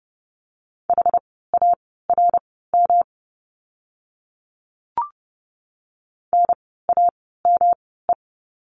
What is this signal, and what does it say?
Morse code, heard as a keyed tone, about 30 wpm, spaced as for 17 wpm, 700 Hz. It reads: HARM NAME
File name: course / 30:10wpm / 108 Lesson 18f - Letter m - Words 30wpm